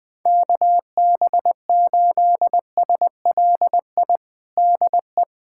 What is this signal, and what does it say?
Morse code, heard as a keyed tone, about 21 wpm, 700 Hz. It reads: KB8SLI DE
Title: USA Callsign DE Callsign - 6 Characters - 20wpm